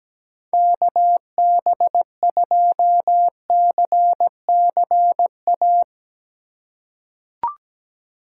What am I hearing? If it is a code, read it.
Morse code, heard as a keyed tone, about 17 wpm, 700 Hz. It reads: KB2CCA
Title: USA Call Signs - 6 Characters - 8x Character Spacing - 17wpm